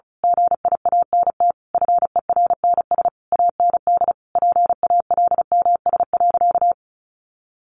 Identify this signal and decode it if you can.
Morse code, heard as a keyed tone, about 34 wpm, 700 Hz. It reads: GIANT FERNS AND PALMS.
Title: A Sound Of Thunder 35wpm